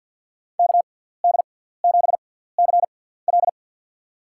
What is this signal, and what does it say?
Morse code, heard as a keyed tone, about 53 wpm, spaced as for 17 wpm, 700 Hz. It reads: KD7CL
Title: USA Call Signs - 6x Character Spacing - 50wpm